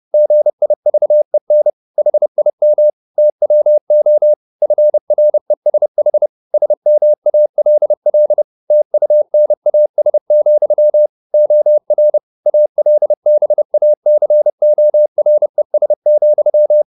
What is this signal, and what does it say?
Morse code, heard as a keyed tone, about 30 wpm, 600 Hz. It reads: GIVEN HIM TWO FRESH SMALL TUNAS, OR ALBACORES,
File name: the old man and the sea / the old man and the sea 1 30wpm